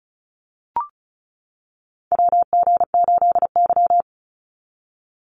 Morse code, heard as a keyed tone, about 34 wpm, 700 Hz. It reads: WG8Y
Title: NA SOTA Chaser Callsigns 35wpm